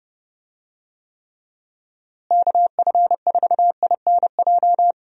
Morse code, heard as a keyed tone, about 30 wpm, 700 Hz. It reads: KF4INJ
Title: USA Call Signs - 6 Characters - Repeated 3x - 30wpm